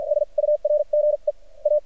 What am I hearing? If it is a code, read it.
Morse code, heard as a keyed tone, about 53 wpm, spaced as for 42 wpm, 600 Hz. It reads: SURGE A
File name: archived / CRSnet